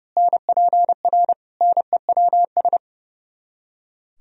Morse code, heard as a keyed tone, about 32 wpm, 700 Hz. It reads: NPR NEWS
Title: Speed-Racing - Sets of 5 Words 30wpm